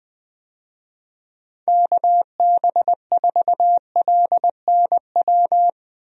Morse code, heard as a keyed tone, about 20 wpm, 700 Hz. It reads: KB4LNW
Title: USA Call Signs - 6 Characters - Repeated 2x - 20wpm